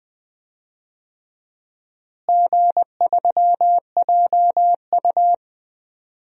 Morse code, heard as a keyed tone, about 20 wpm, 700 Hz. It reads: G3JU
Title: DXCC-100 Call Signs - 4 Characters - Repeated 2x - 20wpm